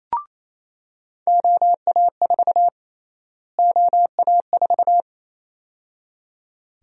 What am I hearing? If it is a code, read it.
Morse code, heard as a keyed tone, about 28 wpm, 700 Hz. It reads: OA4 OA4
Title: USA Call Sign Prefixes - Repeated 2x - 28wpm